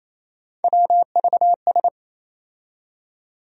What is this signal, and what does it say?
Morse code, heard as a keyed tone, about 28 wpm, 700 Hz. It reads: WVS
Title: USA Call Sign Postfixes - 3 Letters 28wpm